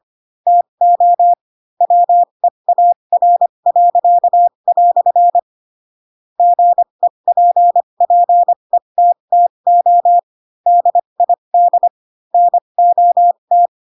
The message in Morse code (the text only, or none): TO WEAR." GEPPETTO DID NOT